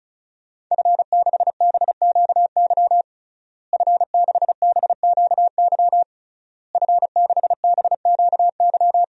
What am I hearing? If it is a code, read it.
Morse code, heard as a keyed tone, about 34 wpm, 700 Hz. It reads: F6BQY F6BQY F6BQY
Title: DXCC-100 Call Signs - 5 Characters - Repeated 3x - 35wpm